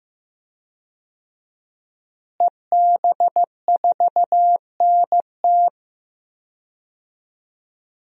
Morse code, heard as a keyed tone, about 15 wpm, 700 Hz. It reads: EB4NT
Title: DXCC-100 Call Signs - 5 Characters - Repeated 2x - 15wpm